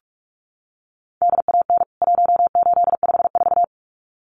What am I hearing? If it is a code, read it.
Morse code, heard as a keyed tone, about 44 wpm, 700 Hz. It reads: DAN 1854
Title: CWT Exchange 45wpm